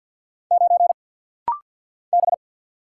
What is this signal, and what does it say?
Morse code, heard as a keyed tone, about 50 wpm, 700 Hz. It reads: 9 B
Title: Single Letter-Number - Rapid-Fire 50wpm